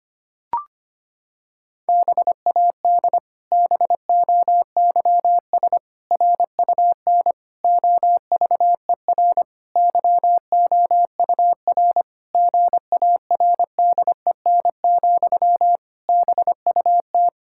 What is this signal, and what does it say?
Morse code, heard as a keyed tone, about 25 wpm, 700 Hz. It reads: BAD BOYS RUN OVER YOUR GARDEN, BUT
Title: Pangram Challenge v2 - 25wpm